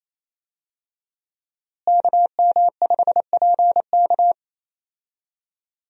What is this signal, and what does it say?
Morse code, heard as a keyed tone, about 28 wpm, 700 Hz. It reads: KM5PK
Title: USA Call Signs - 5 Characters - Repeated 3x - 28wpm